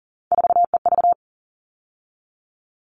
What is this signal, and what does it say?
Morse code, heard as a keyed tone, about 40 wpm, 700 Hz. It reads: 4EV
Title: All Cut Numbers - 3-Digit 40wpm